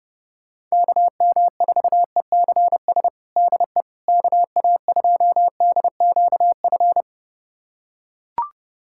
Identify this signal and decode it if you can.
Morse code, heard as a keyed tone, about 30 wpm, 700 Hz. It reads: KM4ECS DE KA2DQF
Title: USA Callsign DE Callsign - 6 Characters - 30wpm